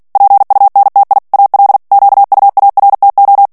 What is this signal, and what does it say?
Morse code, heard as a keyed tone, about 48 wpm, 800 Hz. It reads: PUNTI AL QUARTO
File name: ITALIEN texte clair 2 vitesse 45 mots minute